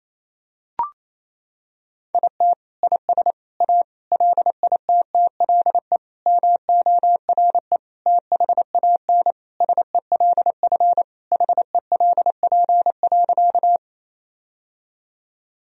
Morse code, heard as a keyed tone, about 28 wpm, 700 Hz. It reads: IT IS A LITTLE MORE THAN SELF HELP.
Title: Sentences from Top 1000 Words - Review Part 1 - 28wpm